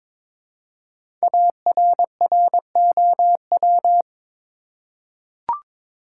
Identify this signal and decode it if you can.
Morse code, heard as a keyed tone, about 22 wpm, 700 Hz. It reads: ARROW
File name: 5 Letter Words - First 3 Letters Given - 22wpm